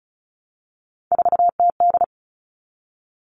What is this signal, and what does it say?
Morse code, heard as a keyed tone, about 34 wpm, 700 Hz. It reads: 4TD
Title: All Cut Numbers - 3-Digit 35wpm